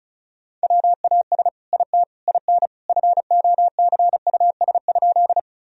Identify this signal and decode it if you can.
Morse code, heard as a keyed tone, about 34 wpm, 700 Hz. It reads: WAS IT IN FOCUS?
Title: Sentences from Top 1000 Words - Review Part 4 - 35wpm